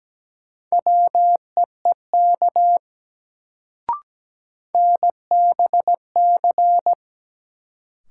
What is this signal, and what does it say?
Morse code, heard as a keyed tone, about 17 wpm, 700 Hz. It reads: WEEK NBC
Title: Sets of 5 Words Encore - New Word Drill - 17wpm